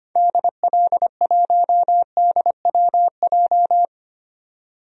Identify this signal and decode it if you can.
Morse code, heard as a keyed tone, about 25 wpm, 700 Hz. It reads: DL1DWJ
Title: DXCC-100 Call Signs - 6 Characters - 25wpm